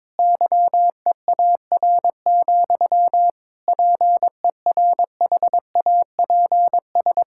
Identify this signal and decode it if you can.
Morse code, heard as a keyed tone, about 22 wpm, 700 Hz. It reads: YEAR, PERHAPS
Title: Sentences from Top 1600 Words 22wpm